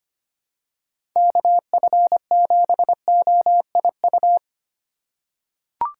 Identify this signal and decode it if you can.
Morse code, heard as a keyed tone, about 25 wpm, 700 Hz. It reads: KF7OIU
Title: USA Call Signs - 6 Characters - 2x Character Spacing - 25wpm